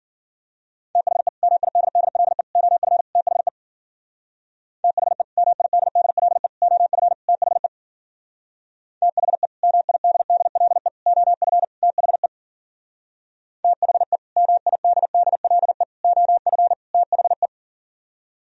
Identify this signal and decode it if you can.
Morse code, heard as a keyed tone, about 52 wpm, 700 Hz. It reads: THE MIDDLE OF THE THE MIDDLE OF THE THE MIDDLE OF THE THE MIDDLE OF THE